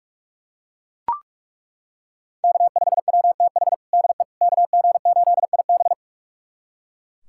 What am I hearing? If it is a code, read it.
Morse code, heard as a keyed tone, about 41 wpm, 700 Hz. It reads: K5WTH DE KG8IB
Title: USA Callsign DE Callsign - 5 Characters - 45wpm